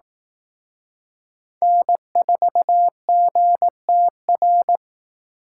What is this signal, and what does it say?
Morse code, heard as a keyed tone, about 18 wpm, 700 Hz. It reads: N4GTR
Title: Speed Racing - USA 1x3 Call Signs 15wpm